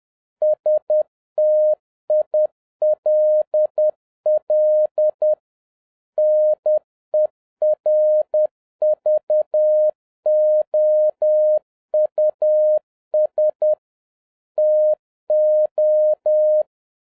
Morse code, heard as a keyed tone, about 10 wpm, 610 Hz. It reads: STILL NERVOUS TO